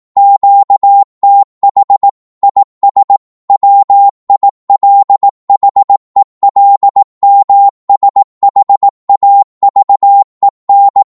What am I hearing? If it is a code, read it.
Morse code, heard as a keyed tone, about 18 wpm, 800 Hz. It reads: QTH IS WILHELMSHAVEN
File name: Morse / qso-18wpm